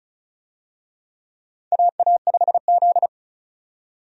Morse code, heard as a keyed tone, about 34 wpm, 700 Hz. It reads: AA5Z
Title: USA Call Signs - 4 Characters - 35wpm